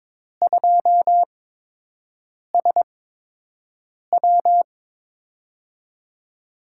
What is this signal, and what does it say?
Morse code, heard as a keyed tone, about 22 wpm, spaced as for 6 wpm, 700 Hz. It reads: 2SW
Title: USA Call Signs - 4 Characters - 8x Character Spacing - 22wpm